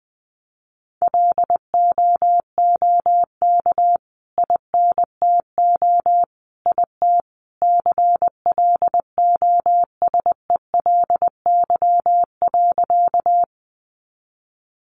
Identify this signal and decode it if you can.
Morse code, heard as a keyed tone, about 20 wpm, 700 Hz. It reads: LOOK INTO IT CLOSELY.
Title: Sentences from Top 1400 Words 20wpm